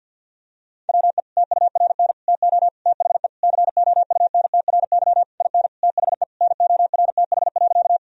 Morse code, heard as a keyed tone, about 50 wpm, 700 Hz. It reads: WE TURN TO THE COUNTRY IN THE NORTH.
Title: Sentences from Top 1000 Words - Review Part 1 - 50wpm